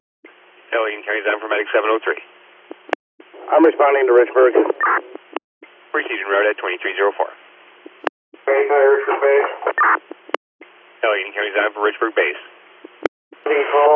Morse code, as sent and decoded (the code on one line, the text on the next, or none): none